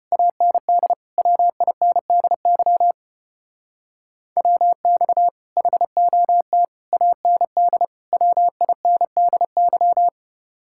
Morse code, heard as a keyed tone, about 30 wpm, 700 Hz. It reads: AND WINDY WX HOT AND WINDY
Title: Speed-Racing Common QSO Phrases Encore - 25wpm